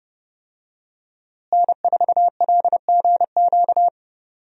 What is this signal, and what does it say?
Morse code, heard as a keyed tone, about 30 wpm, 700 Hz. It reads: N4LGQ
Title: USA Call Signs - 5 Characters - Repeated 2x - 30wpm